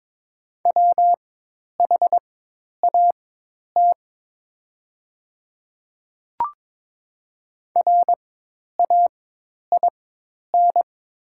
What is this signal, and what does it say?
Morse code, heard as a keyed tone, about 22 wpm, spaced as for 10 wpm, 700 Hz. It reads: WHAT RAIN
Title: 4x Character Spacing - Top 300 Words - 22wpm